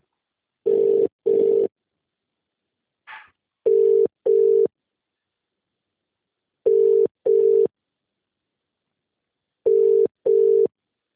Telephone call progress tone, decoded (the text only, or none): ringback tone